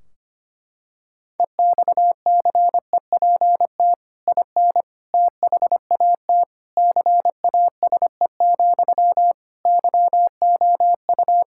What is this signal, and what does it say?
Morse code, heard as a keyed tone, about 25 wpm, 700 Hz. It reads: EXCEPT IN THAT CASE, YOU
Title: Sentences from Top 1600 Words 25wpm